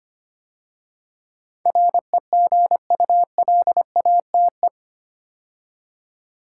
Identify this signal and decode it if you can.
Morse code, heard as a keyed tone, about 25 wpm, 700 Hz. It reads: REGULATE
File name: 8 Letter Words - First 3 Letters Given - 25wpm